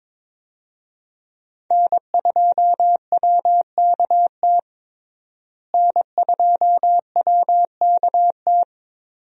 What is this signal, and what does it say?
Morse code, heard as a keyed tone, about 22 wpm, 700 Hz. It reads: N2WKT N2WKT
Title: USA Call Signs - 5 Characters - Repeated 3x - 22wpm